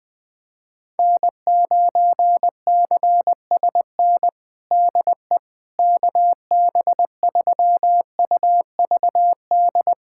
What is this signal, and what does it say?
Morse code, heard as a keyed tone, about 20 wpm, 700 Hz. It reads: N9CSN DE KB3UVD
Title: USA Callsign DE Callsign 20wpm